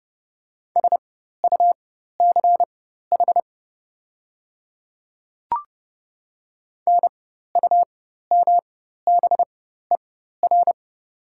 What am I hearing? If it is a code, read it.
Morse code, heard as a keyed tone, about 30 wpm, spaced as for 14 wpm, 700 Hz. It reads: SUCH NUMBER
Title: 4x Character Spacing - Top 300 Words - 30wpm